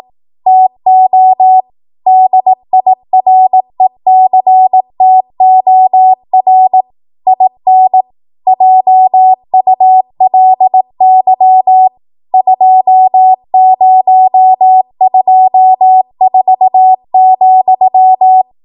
Morse code, heard as a keyed tone, about 18 wpm, 750 Hz. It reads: TO DIRECTOR IN JULY 2024,